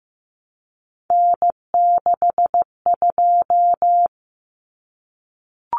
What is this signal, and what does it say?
Morse code, heard as a keyed tone, about 15 wpm, 700 Hz. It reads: N62
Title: Common Cut Numbers - 3-Digit 15wpm